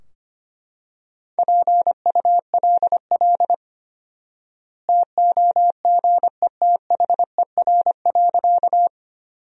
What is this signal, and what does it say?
Morse code, heard as a keyed tone, about 25 wpm, 700 Hz. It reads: PULL TOGETHER.
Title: 4x Word-Spacing - Sentences from Top 400 Words - 25wpm